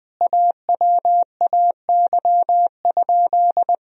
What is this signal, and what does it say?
Morse code, heard as a keyed tone, about 20 wpm, 700 Hz. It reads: AWAY?
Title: Sentences from Top 1000 Words - Review Part 1 - 20wpm